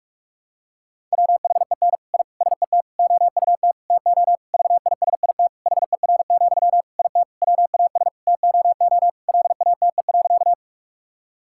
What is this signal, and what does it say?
Morse code, heard as a keyed tone, about 41 wpm, 700 Hz. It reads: WHEN I SET OUT TO VISIT HER, IT WAS TOO LATE.
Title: Sentences from Top 1000 Words - Review Part 6 - 45wpm